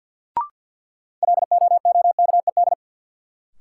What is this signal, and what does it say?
Morse code, heard as a keyed tone, about 49 wpm, 700 Hz. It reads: LOOKED
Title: Sentences from Top 800 Words - New Word Drill - 50wpm